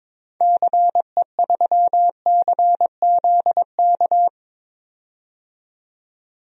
Morse code, heard as a keyed tone, about 22 wpm, 700 Hz. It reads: CE3CZK
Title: DXCC-100 Call Signs - 6 Characters - 22wpm